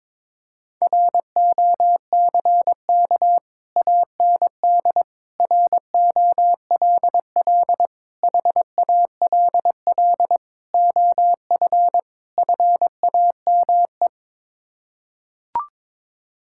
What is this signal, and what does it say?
Morse code, heard as a keyed tone, about 22 wpm, 700 Hz. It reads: ROCK AND ROLL HALL OF FAME
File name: practice / 22wpm / Sets of 6 Words - 22wpm